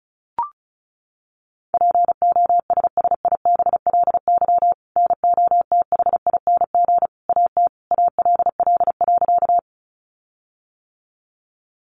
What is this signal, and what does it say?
Morse code, heard as a keyed tone, about 34 wpm, 700 Hz. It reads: POSSIBLY NOTHING AT ALL.
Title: Sentences from Top 1300 Words 35wpm